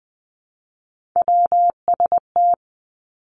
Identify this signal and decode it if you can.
Morse code, heard as a keyed tone, about 20 wpm, 700 Hz. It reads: WST
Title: Three Challenging Letter-Number Combinations 20wpm